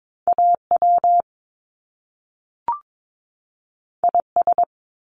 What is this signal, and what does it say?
Morse code, heard as a keyed tone, about 22 wpm, 700 Hz. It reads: AW IS
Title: Top Two Letter Combinations in Words 22wpm